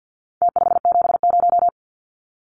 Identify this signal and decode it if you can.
Morse code, heard as a keyed tone, about 49 wpm, 700 Hz. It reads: T570
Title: Numbers - Contest Style - 4-Digit 50wpm